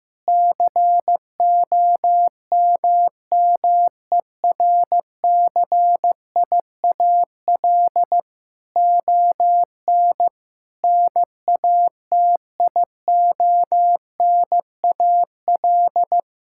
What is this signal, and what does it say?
Morse code, heard as a keyed tone, about 16 wpm, 700 Hz. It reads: COMMERCIAL ON NATIONAL
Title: Sentences from Top 1000 Words - Review Part 5 - 15wpm